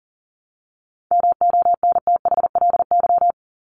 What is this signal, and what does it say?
Morse code, heard as a keyed tone, about 40 wpm, 700 Hz. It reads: MONTHLY
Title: LY Suffix - Top 100 Words with Base Words 40wpm